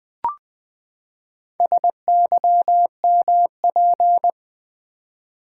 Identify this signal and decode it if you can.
Morse code, heard as a keyed tone, about 20 wpm, 700 Hz. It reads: SYMP
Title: Build A Word - 4 Letters At A Time - 11-Letter Words - 20wpm